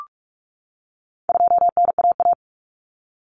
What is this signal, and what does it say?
Morse code, heard as a keyed tone, about 44 wpm, 700 Hz. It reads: 2NAA
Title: Common Cut Numbers - 4-Digit 45wpm